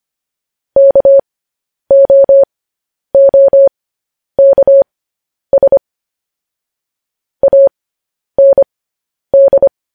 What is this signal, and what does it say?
Morse code, heard as a keyed tone, about 25 wpm, spaced as for 10 wpm, 550 Hz. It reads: KOOKS AND
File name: podcast / News headlines-1x-25x10